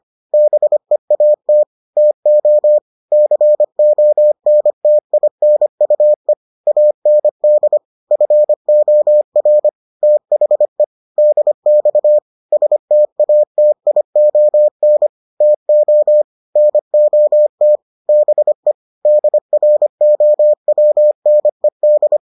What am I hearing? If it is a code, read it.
Morse code, heard as a keyed tone, about 25 wpm, 600 Hz. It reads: BEAT TO CONTINUE AND FOR THE DX STATION TO NOT BE DROWNED